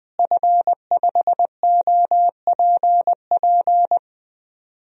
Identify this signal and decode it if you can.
Morse code, heard as a keyed tone, about 20 wpm, 700 Hz. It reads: F5OPP
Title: DXCC-100 Call Signs - 5 Characters - Repeated 2x - 20wpm